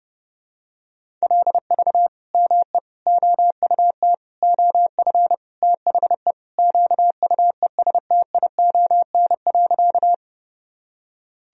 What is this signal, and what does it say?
Morse code, heard as a keyed tone, about 30 wpm, 700 Hz. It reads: LV ME OUT OF THE QUESTION.
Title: Sentences from Top 400 Words and Abbreviations 30wpm